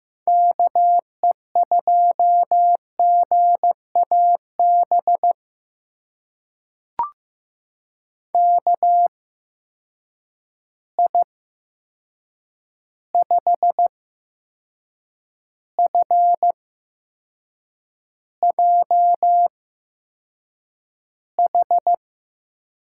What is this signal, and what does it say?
Morse code, heard as a keyed tone, about 15 wpm, spaced as for 6 wpm, 700 Hz. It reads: KE2GAB KI5FJH
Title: USA Call Signs - 6 Characters - 8x Character Spacing - 15wpm